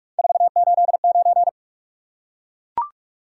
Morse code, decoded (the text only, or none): 489